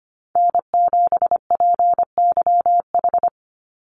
Morse code, heard as a keyed tone, about 25 wpm, 700 Hz. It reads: N7PYH
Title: USA Call Signs 25wpm